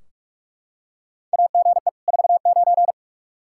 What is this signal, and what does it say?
Morse code, heard as a keyed tone, about 46 wpm, 700 Hz. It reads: AGE 49
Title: Practice QSO - Exchange 3 - More Info - 45wpm